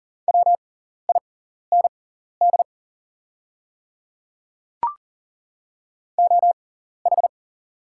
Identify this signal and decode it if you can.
Morse code, heard as a keyed tone, about 40 wpm, spaced as for 14 wpm, 700 Hz. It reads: WIND OH
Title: 6x Character Spacing - Top 500 Words - 40wpm